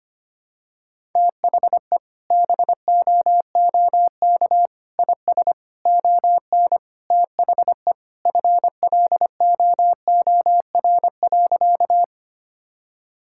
Morse code, heard as a keyed tone, about 25 wpm, 700 Hz. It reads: THE BOOK IS ON THE FLOOR.
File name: practice / 25wpm / Sentences with 3-Word Phrases - 25wpm